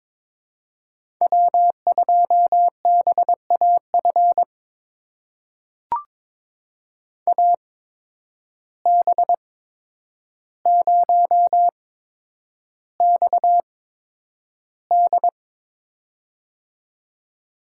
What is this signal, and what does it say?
Morse code, heard as a keyed tone, about 22 wpm, spaced as for 9 wpm, 700 Hz. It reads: W2BAF AB0XD
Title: USA Call Signs - 5 Characters - 8x Character Spacing - 22wpm